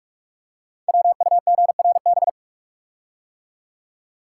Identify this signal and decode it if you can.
Morse code, heard as a keyed tone, about 44 wpm, 700 Hz. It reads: WUGRB
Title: Code Groups - 5 Letters - 45wpm